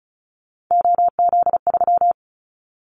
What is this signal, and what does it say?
Morse code, heard as a keyed tone, about 36 wpm, 700 Hz. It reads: OZ3
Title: Callsign Prefixes for Top 100 DXCC - 35wpm